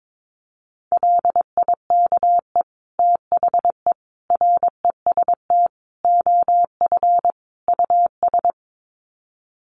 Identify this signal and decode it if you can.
Morse code, heard as a keyed tone, about 22 wpm, 700 Hz. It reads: LIKE THE REST OF US
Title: Sets of 5 Words Encore2 22wpm